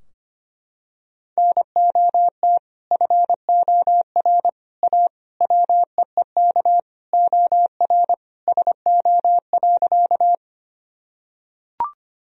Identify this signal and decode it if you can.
Morse code, heard as a keyed tone, about 25 wpm, 700 Hz. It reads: NOT FOR A WEEK OR SO.
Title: Sentences from Top 1000 Words - Review Part 2 - 25wpm